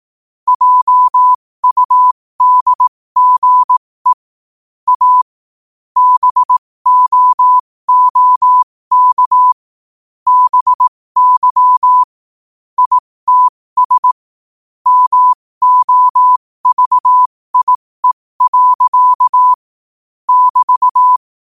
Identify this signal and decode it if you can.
Morse code, heard as a keyed tone, about 18 wpm, spaced as for 15 wpm, 1000 Hz. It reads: JUDGE A BOOK BY ITS MOVIE. <BT>
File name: qotdfiles / 15wpm